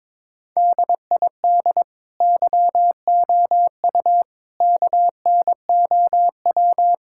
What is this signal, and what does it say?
Morse code, heard as a keyed tone, about 22 wpm, 700 Hz. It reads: DID YOU KNOW
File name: Sentences with 9-Word Phrases - 22wpm